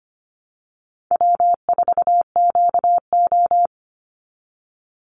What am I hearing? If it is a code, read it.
Morse code, heard as a keyed tone, about 25 wpm, 700 Hz. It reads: W4QO
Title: Call Signs Repeated 3x - 25wpm